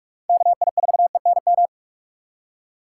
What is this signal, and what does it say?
Morse code, heard as a keyed tone, about 45 wpm, 700 Hz. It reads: KI4ENM
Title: USA Call Signs - 6 Characters - 4x Character Spacing - 45wpm